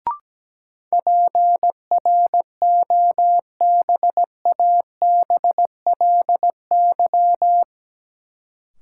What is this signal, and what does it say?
Morse code, heard as a keyed tone, about 17 wpm, 700 Hz. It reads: PROBABLY
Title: Sentences from Top 700 Words - New Word Drill - 17wpm